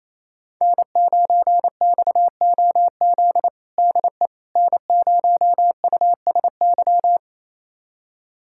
Morse code, heard as a keyed tone, about 28 wpm, 700 Hz. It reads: N9XOZ DE N0USY
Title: USA Callsign DE Callsign - 5 Characters - 28wpm